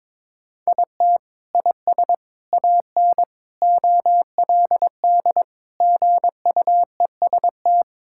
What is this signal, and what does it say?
Morse code, heard as a keyed tone, about 22 wpm, 700 Hz. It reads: IT IS AN OLD GUEST
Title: Sentences from Top 1600 Words 22wpm